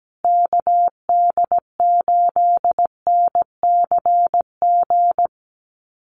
Morse code, heard as a keyed tone, about 17 wpm, 700 Hz. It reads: KD8NCG
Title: USA Call Signs Repeated 3x 17wpm